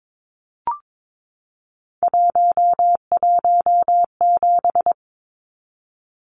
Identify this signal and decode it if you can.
Morse code, heard as a keyed tone, about 22 wpm, 700 Hz. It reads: 117
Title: Numbers - 3-Digit 22wpm